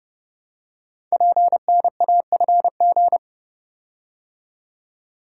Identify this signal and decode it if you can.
Morse code, heard as a keyed tone, about 28 wpm, 700 Hz. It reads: PNAFG
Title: Code Groups - 5 Letters - 30wpm